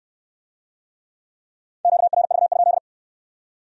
Morse code, heard as a keyed tone, about 68 wpm, 700 Hz. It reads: QRV?